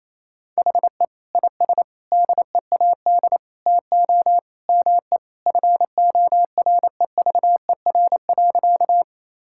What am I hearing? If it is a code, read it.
Morse code, heard as a keyed tone, about 28 wpm, 700 Hz. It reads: HE IS DEAD TO ME FOREVER.
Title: Sentences from Top 1600 Words 28wpm